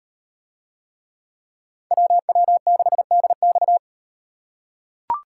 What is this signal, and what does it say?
Morse code, heard as a keyed tone, about 38 wpm, 700 Hz. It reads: WW6DX